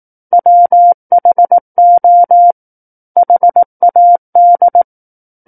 Morse code, heard as a keyed tone, about 18 wpm, 700 Hz. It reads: WHO HAD
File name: Book One / 09-Chapter 9 - 18 WPM